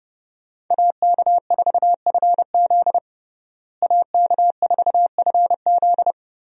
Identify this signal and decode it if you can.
Morse code, heard as a keyed tone, about 30 wpm, 700 Hz. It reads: AK4FZ AK4FZ
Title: USA Call Signs - 5 Characters - Repeated 2x - 30wpm